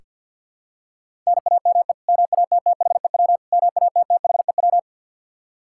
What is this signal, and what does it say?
Morse code, heard as a keyed tone, about 46 wpm, 700 Hz. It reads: NAME MATTHEW MATTHEW
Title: K1USN SST QSO Practice 50wpm